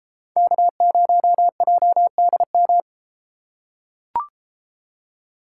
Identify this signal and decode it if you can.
Morse code, heard as a keyed tone, about 30 wpm, 700 Hz. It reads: K0JDM